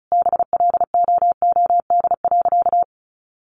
Morse code, heard as a keyed tone, about 34 wpm, 700 Hz. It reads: BLOOD.
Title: Sentences from Top 1100 Words 35wpm